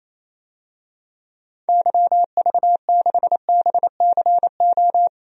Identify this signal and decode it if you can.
Morse code, heard as a keyed tone, about 28 wpm, 700 Hz. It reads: YV6BCO
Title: DXCC-100 Call Signs - 6 Characters - 28wpm